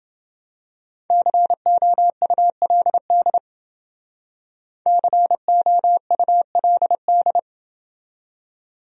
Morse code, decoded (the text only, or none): COULD COULD